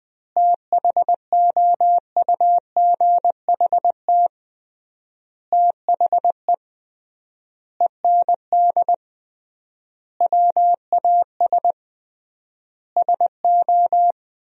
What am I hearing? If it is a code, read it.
Morse code, heard as a keyed tone, about 20 wpm, 700 Hz. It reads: THOUGHT THE END WAS SO
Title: 3x Word-Spacing - Sentences from Top 300 Words - 20wpm